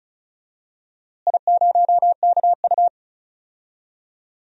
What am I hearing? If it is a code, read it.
Morse code, heard as a keyed tone, about 36 wpm, 700 Hz. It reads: I0KU
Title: DXCC-100 Call Signs - 4 Characters - Repeated 3x - 35wpm